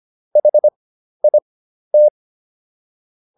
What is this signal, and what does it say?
Morse code, heard as a keyed tone, about 25 wpm, spaced as for 12 wpm, 600 Hz. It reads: HIT